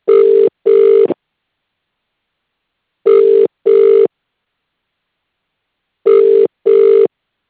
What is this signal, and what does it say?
Ringback tone (UK), 400 + 450 Hz.